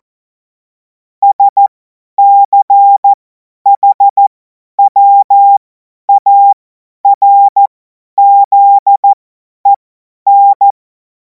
Morse code, heard as a keyed tone, about 14 wpm, spaced as for 10 wpm, 800 Hz. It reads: SCHWARZEN
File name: Morsekurs / Witze / Witz 17. Februar 2026 50 F70